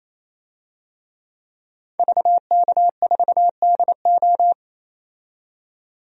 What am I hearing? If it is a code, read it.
Morse code, heard as a keyed tone, about 28 wpm, 700 Hz. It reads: VK4DO
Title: DXCC-100 Call Signs - 5 Characters - 28wpm